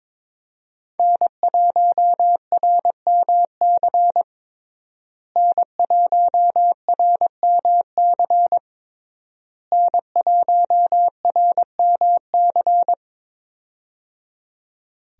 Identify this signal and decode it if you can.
Morse code, heard as a keyed tone, about 22 wpm, 700 Hz. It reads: N1RMC N1RMC N1RMC
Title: USA Call Signs - 5 Characters - Repeated 3x - 22wpm